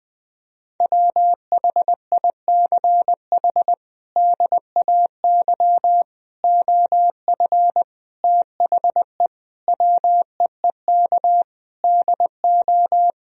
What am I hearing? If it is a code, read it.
Morse code, heard as a keyed tone, about 20 wpm, 700 Hz. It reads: WHICH DAY OF THE WEEK DO
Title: Sentences from Top 1000 Words - Review Part 3 - 20wpm